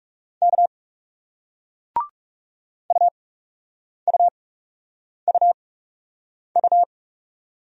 Morse code, heard as a keyed tone, about 40 wpm, spaced as for 8 wpm, 700 Hz. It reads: K UUUU